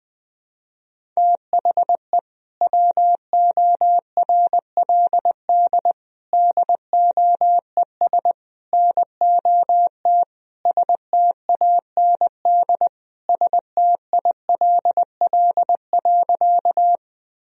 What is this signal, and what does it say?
Morse code, heard as a keyed tone, about 20 wpm, 700 Hz. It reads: THE WORLD DOES NOT STAND STILL.
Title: Sentences from Top 1000 Words - Review Part 1 - 20wpm